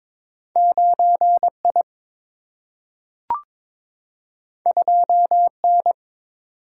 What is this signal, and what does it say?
Morse code, heard as a keyed tone, about 22 wpm, 700 Hz. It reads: 9I 2N
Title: Single-Letter Single-Number 22wpm